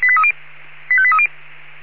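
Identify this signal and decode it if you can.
Five-tone selective call (ZVEI), heard as quick runs of five tones, tones of about 70 ms. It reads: 85830
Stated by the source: ZVEI five-tone call 85839 received on SDR.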